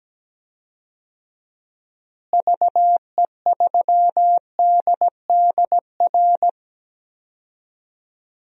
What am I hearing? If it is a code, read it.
Morse code, heard as a keyed tone, about 17 wpm, 700 Hz. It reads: VE3DDR
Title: DXCC-100 Call Signs - 6 Characters - Repeated 3x - 17wpm